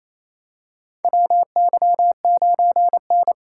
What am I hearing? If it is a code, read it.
Morse code, heard as a keyed tone, about 28 wpm, 700 Hz. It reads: WY9N
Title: USA Call Signs - 4 Characters - 2x Character Spacing - 28wpm